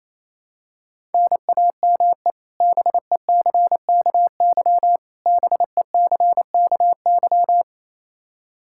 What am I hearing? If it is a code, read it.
Morse code, heard as a keyed tone, about 27 wpm, 700 Hz. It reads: NAME BECKY BECKY
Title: K1USN SST QSO Practice 28wpm